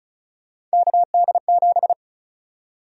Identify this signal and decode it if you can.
Morse code, heard as a keyed tone, about 36 wpm, 700 Hz. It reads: KD7
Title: USA Call Sign Prefixes - Repeated 2x - 35wpm